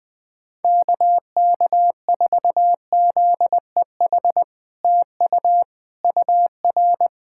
Morse code, heard as a keyed tone, about 20 wpm, 700 Hz. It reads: KK4ZEH TU UR
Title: SOTA QSO Practice 20wpm